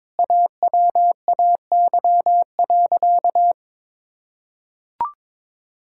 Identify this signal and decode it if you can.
Morse code, heard as a keyed tone, about 22 wpm, 700 Hz. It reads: AWAY.
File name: practice / 22wpm / Sentences with 4-Word Phrases - 22wpm